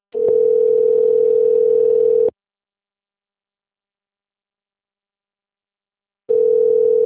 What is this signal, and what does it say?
Ringback tone (North America), 440 + 480 Hz.
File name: rg-1600-unknown- - - .1688